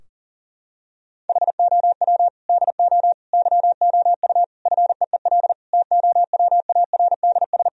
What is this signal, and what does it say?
Morse code, heard as a keyed tone, about 40 wpm, 700 Hz. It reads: HOW DO YOU FEEL TOWARDS
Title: Sentences from Top 1000 Words - Review Part 2 - 40wpm